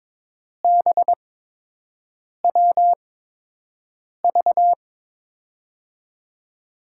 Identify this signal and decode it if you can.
Morse code, heard as a keyed tone, about 22 wpm, spaced as for 6 wpm, 700 Hz. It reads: BWV
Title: USA Call Signs - 6 Characters - 8x Character Spacing - 22wpm